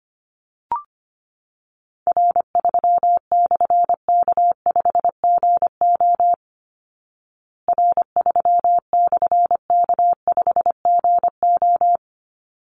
Morse code, heard as a keyed tone, about 25 wpm, 700 Hz. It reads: R3/K5GO R3/K5GO
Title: Call Signs Repeated 3x - 25wpm